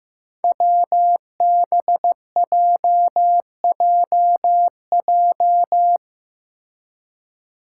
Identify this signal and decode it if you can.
Morse code, heard as a keyed tone, about 15 wpm, 700 Hz. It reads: WBJJJ
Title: Five Challenging Letter-Number Combinations 15wpm